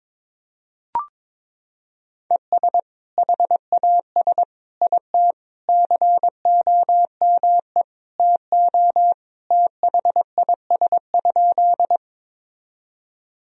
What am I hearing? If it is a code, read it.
Morse code, heard as a keyed tone, about 22 wpm, 700 Hz. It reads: ES HAS IT COME TO THIS?
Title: Sentences from Top 100 Words and Abbreviations 22wpm